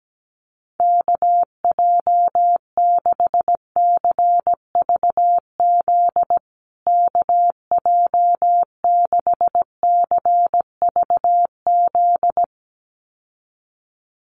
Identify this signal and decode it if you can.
Morse code, heard as a keyed tone, about 17 wpm, 700 Hz. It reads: KJ6CVZ KJ6CVZ
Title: USA Call Signs Repeated 2x 17wpm